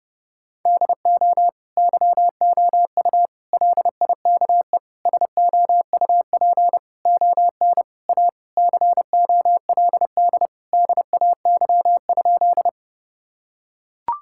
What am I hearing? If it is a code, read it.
Morse code, heard as a keyed tone, about 32 wpm, 700 Hz. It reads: DO YOU LIKE SOUP ON A COLD DAY?
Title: Sentences from Top 1000 Words - Review Part 6 - 30wpm